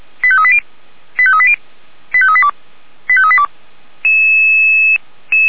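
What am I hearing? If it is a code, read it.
Five-tone selective call (ZVEI), heard as quick runs of five tones, tones of about 70 ms. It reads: 85380 85380 85382 85382
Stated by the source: ZVEI five-tone call received on SDR.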